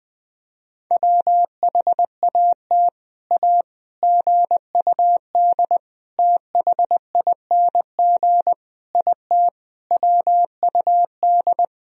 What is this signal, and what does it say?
Morse code, heard as a keyed tone, about 21 wpm, 700 Hz. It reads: WHAT A GUD THING IT WUD
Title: Sentences from Top 200 Words and Abbreviations 20wpm